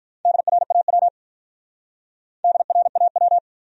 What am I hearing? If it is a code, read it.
Morse code, heard as a keyed tone, about 44 wpm, 700 Hz. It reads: DRAW DRAW
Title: Speed-Racing Top 500 Words 35wpm